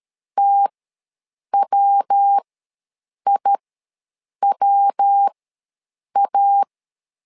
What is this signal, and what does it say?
Morse code, heard as a keyed tone, about 13 wpm, spaced as for 7 wpm, 800 Hz. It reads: TWIWA
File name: LEZIONI / 3 MWRO / 3 MW ws 12wpm 1816